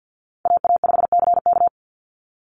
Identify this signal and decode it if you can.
Morse code, heard as a keyed tone, about 50 wpm, 700 Hz. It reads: AA5CK
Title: NA SOTA Chaser Callsigns 50wpm